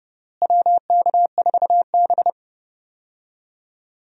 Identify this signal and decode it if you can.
Morse code, heard as a keyed tone, about 30 wpm, 700 Hz. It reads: WK4B
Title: USA Call Signs - 4 Characters - Repeated 3x - 30wpm